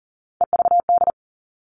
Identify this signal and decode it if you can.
Morse code, heard as a keyed tone, about 40 wpm, 700 Hz. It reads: EVD
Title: All Cut Numbers - 3-Digit 40wpm